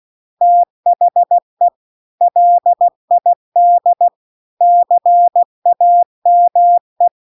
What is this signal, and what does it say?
Morse code, heard as a keyed tone, about 16 wpm, 700 Hz. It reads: THE LID CAME